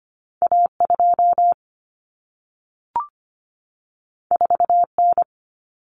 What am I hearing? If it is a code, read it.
Morse code, heard as a keyed tone, about 25 wpm, 700 Hz. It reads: A2 4N
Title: Common Cut Numbers - 2-Digit 25wpm